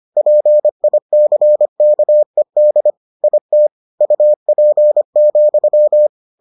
Morse code, heard as a keyed tone, about 25 wpm, 600 Hz. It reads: PICKED IT UP,